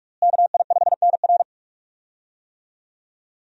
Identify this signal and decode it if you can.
Morse code, heard as a keyed tone, about 46 wpm, 700 Hz. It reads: KI5NR
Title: USA Call Signs - 5 Characters - Repeated 2x - 45wpm